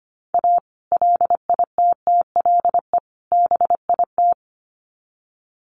Morse code, heard as a keyed tone, about 25 wpm, 700 Hz. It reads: A LITTLE BIT
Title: Sets of 3 Words 25wpm